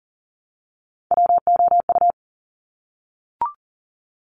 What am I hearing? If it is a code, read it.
Morse code, heard as a keyed tone, about 40 wpm, 700 Hz. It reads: WOU
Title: Top Three Letter Combinations in Words 40wpm